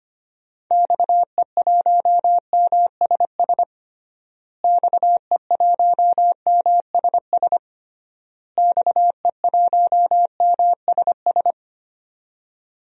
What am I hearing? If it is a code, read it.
Morse code, heard as a keyed tone, about 25 wpm, 700 Hz. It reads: XE1MSS XE1MSS XE1MSS
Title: DXCC-100 Call Signs - 6 Characters - Repeated 3x - 25wpm